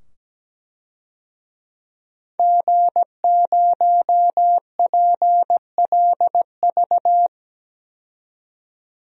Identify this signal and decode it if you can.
Morse code, heard as a keyed tone, about 17 wpm, 700 Hz. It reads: G0PLV
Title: DXCC-100 Call Signs - 5 Characters - 17wpm